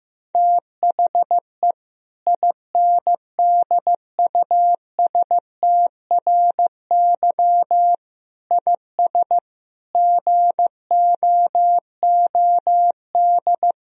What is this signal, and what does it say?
Morse code, heard as a keyed tone, about 15 wpm, 700 Hz. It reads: THE INDUSTRY IS GOOD
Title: Sentences from Top 1000 Words - Review Part 4 - 15wpm